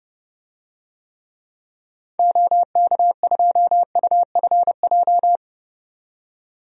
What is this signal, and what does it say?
Morse code, heard as a keyed tone, about 31 wpm, 700 Hz. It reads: OK2UFJ
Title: DXCC-100 Call Signs - 6 Characters - 30wpm